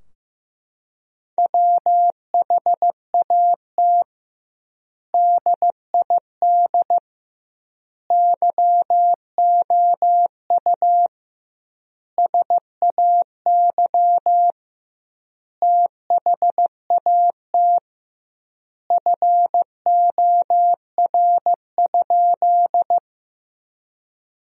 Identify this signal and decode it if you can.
Morse code, heard as a keyed tone, about 15 wpm, 700 Hz. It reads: WHAT DID YOU SAY THAT FOR?
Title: 2x Word-Spacing - Sentences from Top 200 Words - 15wpm